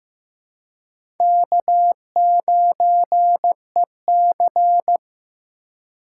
Morse code, heard as a keyed tone, about 15 wpm, 700 Hz. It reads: K9EC
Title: USA Call Signs - 3x Character Spacing - 15wpm